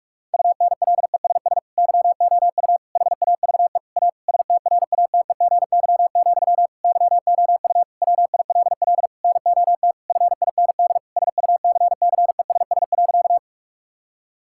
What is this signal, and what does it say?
Morse code, heard as a keyed tone, about 46 wpm, 700 Hz. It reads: UNLESS YOU HAVE A STRATEGY, YOU WILL NOT FIND SUCCESS.
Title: Sentences from Top 1000 Words - Review Part 5 - 45wpm